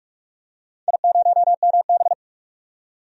Morse code, heard as a keyed tone, about 42 wpm, 700 Hz. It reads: I0MB
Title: DXCC-100 Call Signs - 4 Characters - 45wpm